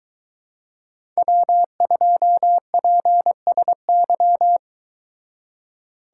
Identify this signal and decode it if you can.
Morse code, heard as a keyed tone, about 22 wpm, 700 Hz. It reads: W2PSY